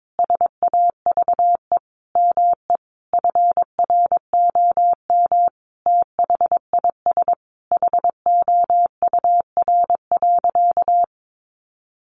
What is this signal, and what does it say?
Morse code, heard as a keyed tone, about 22 wpm, 700 Hz. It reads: SAVE ME FROM THIS HOUR.
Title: Sentences from Top 1100 Words 22wpm